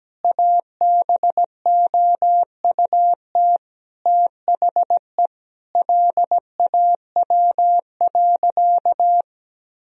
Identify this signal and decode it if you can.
Morse code, heard as a keyed tone, about 17 wpm, 700 Hz. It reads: ABOUT THE LAW.
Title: Sentences from Top 1000 Words - Review Part 3 - 17wpm